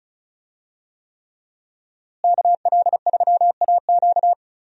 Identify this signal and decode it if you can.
Morse code, heard as a keyed tone, about 36 wpm, 700 Hz. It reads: KL3AQ
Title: USA Call Signs - 5 Characters - Repeated 3x - 35wpm